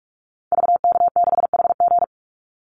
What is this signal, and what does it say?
Morse code, heard as a keyed tone, about 46 wpm, 700 Hz. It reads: VK6HG
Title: Call Signs - 45wpm